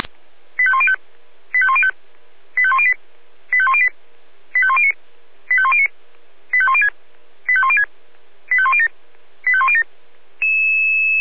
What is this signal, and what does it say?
Five-tone selective call (ZVEI), heard as quick runs of five tones, tones of about 70 ms. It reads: 85105 85105 85108 85108 85109 85109 85106 85106 85107 85107